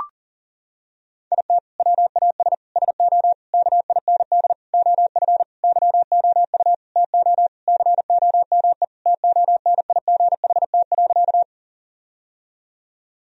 Morse code, heard as a keyed tone, about 40 wpm, 700 Hz. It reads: IT WAS SO KIND OF YOU TO COME TONIGHT.
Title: Sentences from Top 1000 Words - Review Part 3 - 40wpm